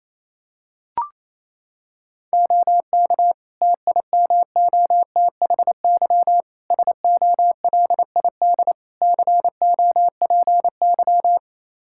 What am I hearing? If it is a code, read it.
Morse code, heard as a keyed tone, about 28 wpm, 700 Hz. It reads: OK TIMOTHY SOLID COPY
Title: K1USN SST QSO Practice 28wpm